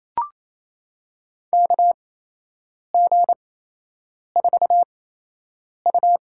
Morse code, heard as a keyed tone, about 28 wpm, spaced as for 8 wpm, 700 Hz. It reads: KG4U
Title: USA Call Signs - 6 Characters - 8x Character Spacing - 28wpm